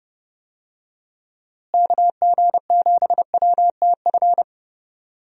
Morse code, heard as a keyed tone, about 30 wpm, 700 Hz. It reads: KG7WTF
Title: USA Call Signs - 6 Characters - Repeated 2x - 30wpm